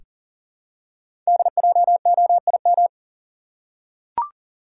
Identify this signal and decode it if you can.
Morse code, heard as a keyed tone, about 40 wpm, 700 Hz. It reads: DJOIM
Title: Code Groups - 5 Letters - 40wpm